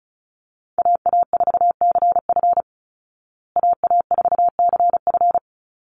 Morse code, heard as a keyed tone, about 34 wpm, 700 Hz. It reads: AA4CF AA4CF
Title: Call Signs Repeated 2x - 35wpm